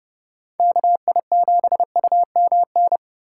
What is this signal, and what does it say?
Morse code, heard as a keyed tone, about 28 wpm, 700 Hz. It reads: KI7UMN